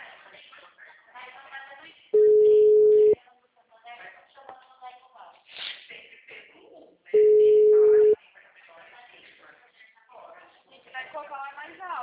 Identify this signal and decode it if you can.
Ringback tone (Europe, 425 Hz), with speech.
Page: Araras - SP